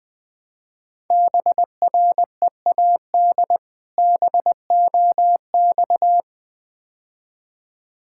Morse code, heard as a keyed tone, about 22 wpm, 700 Hz. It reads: BREAD BOX
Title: Difficult Words - Speed-Racing with Hints - 20wpm